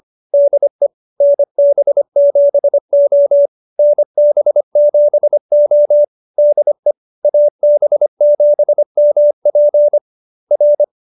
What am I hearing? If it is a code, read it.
Morse code, heard as a keyed tone, about 25 wpm, 590 Hz. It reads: DE NB7O NB7O DE AB7MP R
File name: QSO306 25